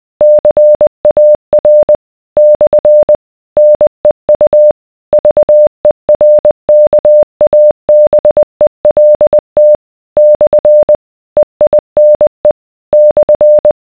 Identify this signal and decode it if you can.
Morse code, heard as a keyed tone, about 20 wpm, 600 Hz. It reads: CAR / NEU VERKABELT / EINE /